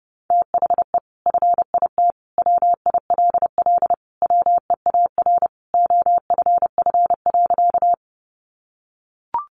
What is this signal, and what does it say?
Morse code, heard as a keyed tone, about 30 wpm, 700 Hz. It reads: THE FIT WILL WEAR OFF.
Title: Sentences from Top 1500 Words 30wpm